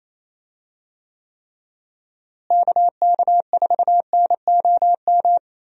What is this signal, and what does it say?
Morse code, heard as a keyed tone, about 27 wpm, 700 Hz. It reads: KK4NOM
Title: USA Call Signs - 6 Characters - Repeated 3x - 28wpm